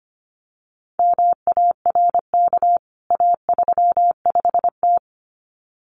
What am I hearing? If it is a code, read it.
Morse code, heard as a keyed tone, about 25 wpm, 700 Hz. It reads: MARK A35T
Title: CWT Exchange 25wpm